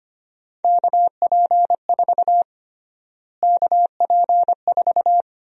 Morse code, heard as a keyed tone, about 25 wpm, 700 Hz. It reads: KP4 KP4
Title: USA Call Sign Prefixes - Repeated 2x - 25wpm